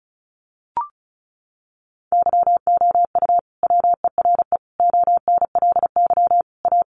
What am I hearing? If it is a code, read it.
Morse code, heard as a keyed tone, about 34 wpm, 700 Hz. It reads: YOU WERE ONLY A
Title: Sentences from Top 1000 Words 35wpm